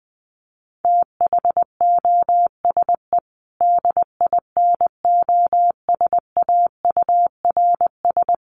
Morse code, heard as a keyed tone, about 20 wpm, 700 Hz. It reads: THOSE DINOSAURS
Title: A Sound Of Thunder 20wpm